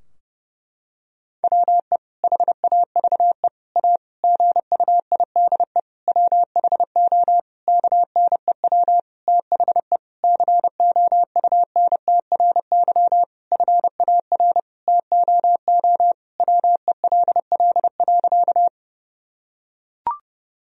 Morse code, heard as a keyed tone, about 30 wpm, 700 Hz. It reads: WE HAVE A GUIDE WHO KNEW THE COUNTRY FAR TOO WELL.
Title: Sentences from Top 1000 Words - Review Part 3 - 30wpm